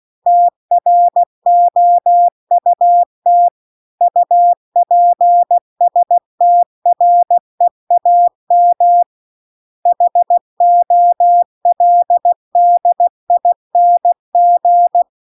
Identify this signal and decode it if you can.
Morse code, heard as a keyed tone, about 16 wpm, 700 Hz. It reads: TROUT UPSTREAM HOLDING